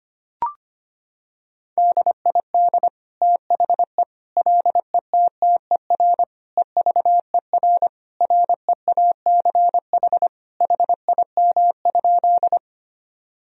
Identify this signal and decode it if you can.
Morse code, heard as a keyed tone, about 25 wpm, 700 Hz. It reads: DID THE LETTER EVER REACH HIM?
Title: Sentences from Top 1000 Words - Review Part 2 - 25wpm